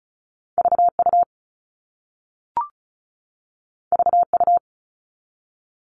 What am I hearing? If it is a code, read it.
Morse code, heard as a keyed tone, about 36 wpm, 700 Hz. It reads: VU VU
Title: All Cut Numbers - 2-Digit 35wpm